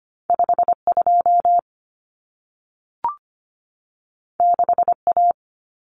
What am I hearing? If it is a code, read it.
Morse code, heard as a keyed tone, about 25 wpm, 700 Hz. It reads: 52 6A
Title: Common Cut Numbers - 2-Digit 25wpm